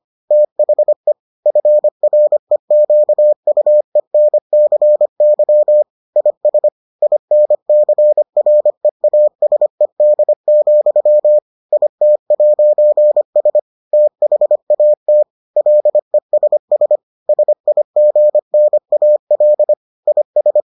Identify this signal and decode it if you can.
Morse code, heard as a keyed tone, about 25 wpm, 600 Hz. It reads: THE FREQUENCY IS INCREASED, IT'S THAT LESS SIGNAL IS